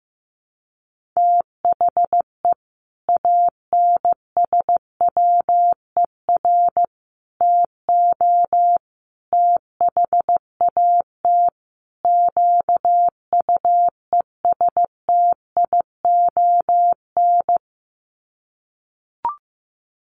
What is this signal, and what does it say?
Morse code, heard as a keyed tone, about 15 wpm, 700 Hz. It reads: THE ANSWER TO THAT QUESTION
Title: Sets of 5 Words Encore2 15wpm